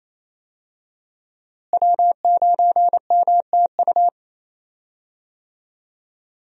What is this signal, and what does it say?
Morse code, heard as a keyed tone, about 28 wpm, 700 Hz. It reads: W9MTU
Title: USA Call Signs - 5 Characters - Repeated 3x - 28wpm